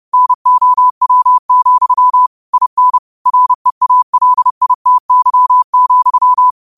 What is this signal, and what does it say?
Morse code, heard as a keyed tone, about 30 wpm, 1000 Hz. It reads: NOW, IN REALITY,
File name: qotdfiles / 30wpm